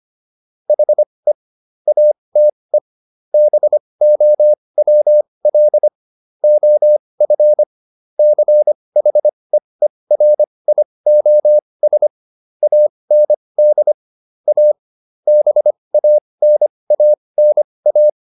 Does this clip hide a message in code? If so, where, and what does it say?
Yes: Morse code from 0.7 to 18.1 s, heard as a keyed tone, about 25 wpm, spaced as for 20 wpm, 600 Hz. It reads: HE ATE BOWL OF CHEERIOS AND A BANANA